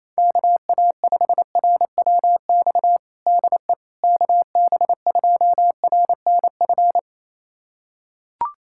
Morse code, heard as a keyed tone, about 28 wpm, 700 Hz. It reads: KA5RWX DE KB2RNF
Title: USA Callsign DE Callsign - 6 Characters - 28wpm